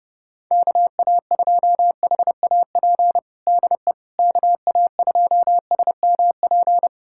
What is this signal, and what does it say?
Morse code, heard as a keyed tone, about 32 wpm, 700 Hz. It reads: KA2HAP DE KA2SMP
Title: USA Callsign DE Callsign - 6 Characters - 30wpm